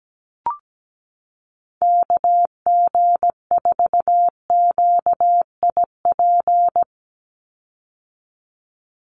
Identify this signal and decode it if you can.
Morse code, heard as a keyed tone, about 17 wpm, 700 Hz. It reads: KG4QIP
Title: USA Call Signs 17wpm